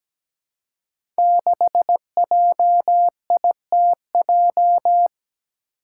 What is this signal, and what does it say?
Morse code, heard as a keyed tone, about 17 wpm, 700 Hz. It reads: 6JITJ
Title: Five Challenging Letter-Number Combinations 17wpm